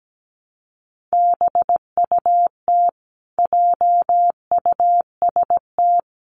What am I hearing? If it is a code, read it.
Morse code, heard as a keyed tone, about 17 wpm, 700 Hz. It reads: BUT JUST
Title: Sentences from Top 1100 Words 17wpm